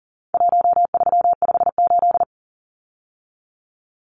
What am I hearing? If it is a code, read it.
Morse code, heard as a keyed tone, about 40 wpm, 700 Hz. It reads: 1358
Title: Numbers - Contest Style - 4-Digit 40wpm